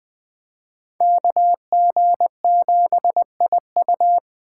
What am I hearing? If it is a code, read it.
Morse code, heard as a keyed tone, about 20 wpm, 700 Hz. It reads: KG7IU
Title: Speed Racing - USA 2x2 Call Signs 17wpm